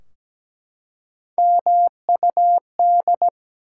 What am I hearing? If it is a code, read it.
Morse code, heard as a keyed tone, about 17 wpm, 700 Hz. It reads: MUD
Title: Three Letter Words - 17wpm